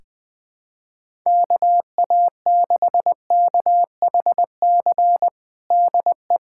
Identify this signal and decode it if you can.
Morse code, heard as a keyed tone, about 21 wpm, 700 Hz. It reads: KA6KHC DE
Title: USA Callsign DE Callsign - 6 Characters - 20wpm